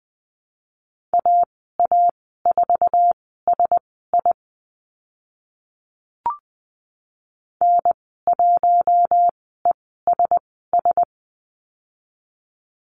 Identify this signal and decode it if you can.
Morse code, heard as a keyed tone, about 20 wpm, spaced as for 14 wpm, 700 Hz. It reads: AA4SI N1ESS
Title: 048 Lesson 9g - Number 4 - Call Signs 20wpm